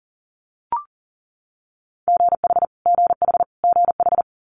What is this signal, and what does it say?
Morse code, heard as a keyed tone, about 40 wpm, 700 Hz. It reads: GH GH GH
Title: Multi-letter Phonemes Repeated 3x 40wpm